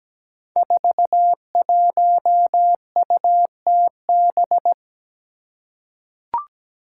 Morse code, heard as a keyed tone, about 17 wpm, 700 Hz. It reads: 41UTB
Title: Five Challenging Letter-Number Combinations 17wpm